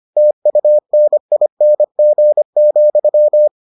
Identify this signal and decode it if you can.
Morse code, heard as a keyed tone, about 25 wpm, 600 Hz. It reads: TUNING,